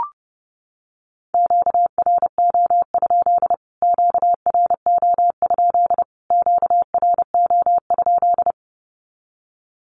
Morse code, heard as a keyed tone, about 30 wpm, 700 Hz. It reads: QRO? QRO? QRO?
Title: Q-Codes Repeated 3x 30wpm